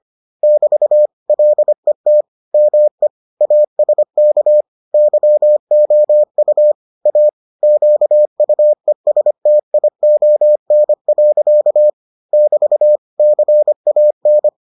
Morse code, heard as a keyed tone, about 25 wpm, 600 Hz. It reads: <BT> LET ME ASK YOU A QUESTION. <BT> CAN